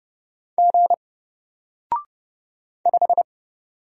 Morse code, heard as a keyed tone, about 30 wpm, 700 Hz. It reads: G 5
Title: Single Letter-Number Alt 30wpm